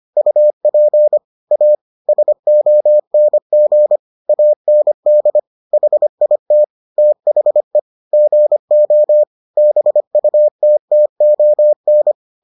Morse code, heard as a keyed tone, about 25 wpm, 600 Hz. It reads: UP A SONG AND HIT THE GO BUTTON